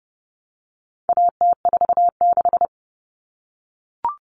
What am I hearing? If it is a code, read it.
Morse code, heard as a keyed tone, about 30 wpm, 700 Hz. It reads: AT46
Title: All Cut Numbers - 4-Digit 30wpm